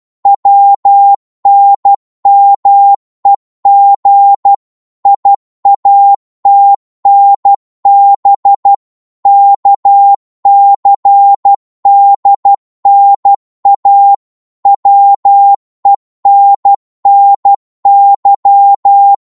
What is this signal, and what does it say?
Morse code, heard as a keyed tone, about 12 wpm, 800 Hz. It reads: WNMEG IATNB KCDNA WENNY
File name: Morse / ltr-12wpm